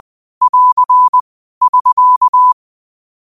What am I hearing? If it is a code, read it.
Morse code, heard as a keyed tone, about 20 wpm, 1000 Hz. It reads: <AR> <SK>
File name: qotdfiles / 20wpm